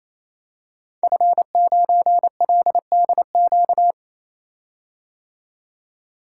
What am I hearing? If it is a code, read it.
Morse code, heard as a keyed tone, about 29 wpm, 700 Hz. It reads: F9LDQ
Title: Code Groups - 5 Letter:Numbers - 28wpm